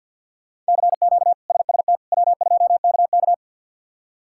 Morse code, heard as a keyed tone, about 51 wpm, 690 Hz. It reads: CQ SST W2KK
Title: K1USN SST QSO Practice - 50wpm